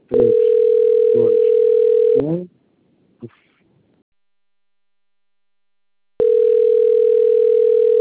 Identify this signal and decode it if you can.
Ringback tone (North America), 440 + 480 Hz, with speech.
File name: rg-1600-unknown- - - .79250